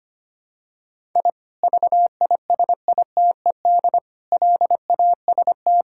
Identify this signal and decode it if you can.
Morse code, heard as a keyed tone, about 25 wpm, 700 Hz. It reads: I VISITED LAST
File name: practice / 25wpm / Sentences with 2-Word Phrases - 25wpm